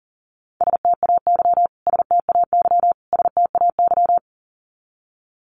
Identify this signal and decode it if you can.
Morse code, heard as a keyed tone, about 40 wpm, 700 Hz. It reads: STAY STAY STAY
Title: Top 500 Words Repeated 40wpm